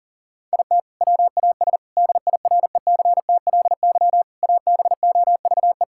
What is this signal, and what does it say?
Morse code, heard as a keyed tone, about 40 wpm, 700 Hz. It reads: IT WAS DIRECTLY ABOVE
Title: Sentences from Top 1000 Words - Review Part 5 - 40wpm